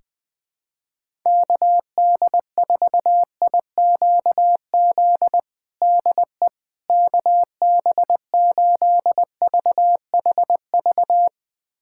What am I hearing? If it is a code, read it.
Morse code, heard as a keyed tone, about 20 wpm, 700 Hz. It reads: KD4IQZ DE KB8VHV
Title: USA Callsign DE Callsign - 6 Characters - 20wpm